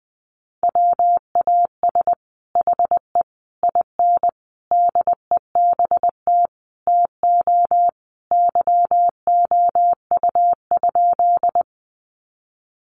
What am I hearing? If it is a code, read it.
Morse code, heard as a keyed tone, about 20 wpm, 700 Hz. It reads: WAS HE IN DEBT TO YOU?
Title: Sentences from Top 1300 Words 20wpm